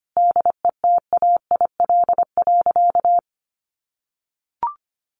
Morse code, heard as a keyed tone, about 25 wpm, 700 Hz. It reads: DETAIL.
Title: Sentences from Top 1500 Words 25wpm